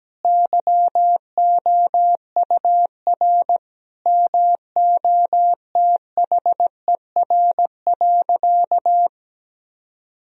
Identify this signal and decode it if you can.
Morse code, heard as a keyed tone, about 17 wpm, 700 Hz. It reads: YOUR MOTHER.
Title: Sentences from Top 1000 Words - Review Part 2 - 17wpm